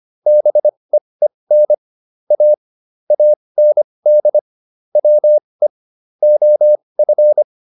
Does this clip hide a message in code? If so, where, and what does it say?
Yes: Morse code from 0.3 to 7.4 s, heard as a keyed tone, about 25 wpm, spaced as for 20 wpm, 600 Hz. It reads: BEEN A AND WE OF